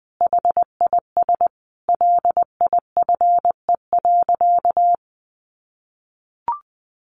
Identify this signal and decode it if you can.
Morse code, heard as a keyed tone, about 20 wpm, 700 Hz. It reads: HIS LIFE.
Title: Sentences from Top 1300 Words 20wpm